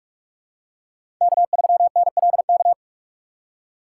Morse code, heard as a keyed tone, about 46 wpm, 700 Hz. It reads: K3NLK
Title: USA Call Signs - 5 Characters - 4x Character Spacing - 45wpm